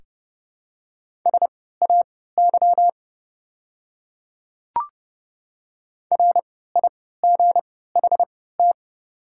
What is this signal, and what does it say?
Morse code, heard as a keyed tone, about 30 wpm, spaced as for 17 wpm, 690 Hz. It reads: SAY RIGHT
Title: 3x Character Spacing - Top 200 Words - 30wpm